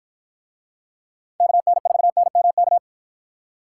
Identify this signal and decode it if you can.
Morse code, heard as a keyed tone, about 49 wpm, 700 Hz. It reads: KN4NMK